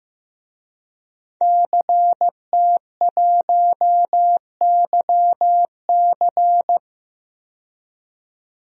Morse code, heard as a keyed tone, about 15 wpm, 700 Hz. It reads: CT1YC
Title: DXCC-100 Call Signs - 5 Characters - 15wpm